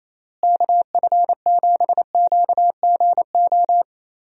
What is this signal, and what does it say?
Morse code, heard as a keyed tone, about 28 wpm, 700 Hz. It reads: KF7QGO
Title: USA Call Signs - 6 Characters - Repeated 2x - 28wpm